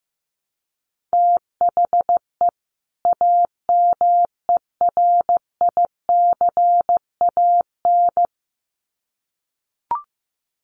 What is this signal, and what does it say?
Morse code, heard as a keyed tone, about 15 wpm, 700 Hz. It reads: THE AMERICAN
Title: Sets of 2 Words 15wpm